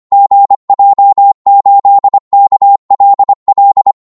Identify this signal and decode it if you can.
Morse code, heard as a keyed tone, about 25 wpm, 800 Hz. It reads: GJ8KLL
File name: Morse / qso-25wpm